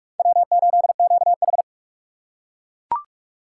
Morse code, heard as a keyed tone, about 41 wpm, 700 Hz. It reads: W8QH
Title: USA Call Signs - 4 Characters - 3x Character Spacing - 45wpm